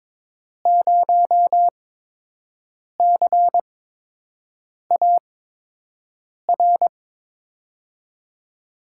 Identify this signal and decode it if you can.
Morse code, heard as a keyed tone, about 22 wpm, spaced as for 6 wpm, 700 Hz. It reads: 0CAR
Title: USA Call Signs - 6 Characters - 8x Character Spacing - 22wpm